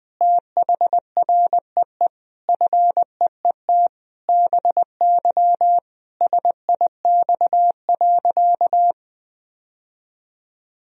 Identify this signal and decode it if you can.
Morse code, heard as a keyed tone, about 20 wpm, 700 Hz. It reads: THREE FEET BY SIX.
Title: Sentences from Top 1000 Words - Review Part 2 - 20wpm